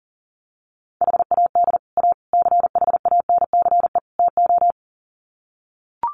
Morse code, heard as a keyed tone, about 40 wpm, 700 Hz. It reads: HAD A CHANCE TO
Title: Sets of 4 Words 40wpm